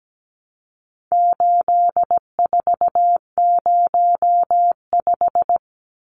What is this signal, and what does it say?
Morse code, heard as a keyed tone, about 17 wpm, 700 Hz. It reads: 8405
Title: Numbers - 4-Digit 17wpm